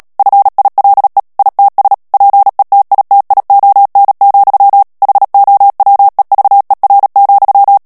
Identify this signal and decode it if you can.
Morse code, heard as a keyed tone, about 37 wpm, 800 Hz. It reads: FILE ITS PETITION, HOWEVER,
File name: morse mp3 / ANGLAIS texte clair 3 vitesse 35 mots minute